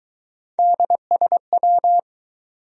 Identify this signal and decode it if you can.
Morse code, heard as a keyed tone, about 23 wpm, 700 Hz. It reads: DSW